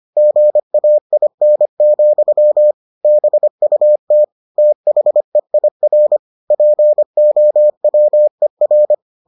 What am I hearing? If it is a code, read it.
Morse code, heard as a keyed tone, about 25 wpm, 600 Hz. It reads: GAIN, BUT THEIR POWER